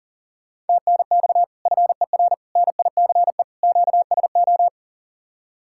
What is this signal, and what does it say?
Morse code, heard as a keyed tone, about 49 wpm, 700 Hz. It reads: TNX FER NICE QSO
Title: Speed-Racing Common QSO Phrases - 40wpm